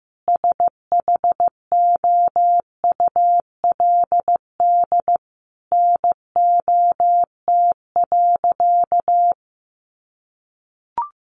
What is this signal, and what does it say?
Morse code, heard as a keyed tone, about 15 wpm, 700 Hz. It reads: SHOULD NOT.
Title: Sentences from Top 900 Words 15wpm